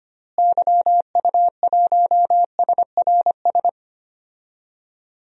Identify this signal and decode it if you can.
Morse code, heard as a keyed tone, about 25 wpm, 700 Hz. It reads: YU1SRS
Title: DXCC-100 Call Signs - 6 Characters - Repeated 2x - 25wpm